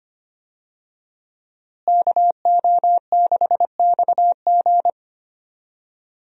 Morse code, heard as a keyed tone, about 25 wpm, 700 Hz. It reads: KO6XG
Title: USA Call Signs - 5 Characters - Repeated 3x - 25wpm